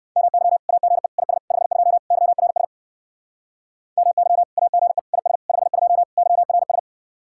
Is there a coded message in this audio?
Yes: Morse code from 0.2 to 6.8 s, heard as a keyed tone, about 62 wpm, 700 Hz. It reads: MY AGE IS 52 YRS MY AGE IS 52 YRS